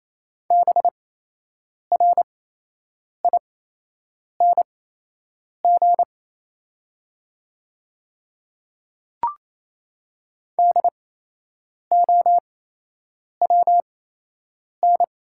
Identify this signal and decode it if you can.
Morse code, heard as a keyed tone, about 28 wpm, spaced as for 8 wpm, 700 Hz. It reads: BRING DOWN
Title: 8x Character Spacing - Top 500 Words - 28wpm